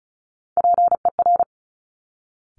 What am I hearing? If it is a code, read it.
Morse code, heard as a keyed tone, about 36 wpm, 700 Hz. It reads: PER
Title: Top Three Letter Combinations in Words 35wpm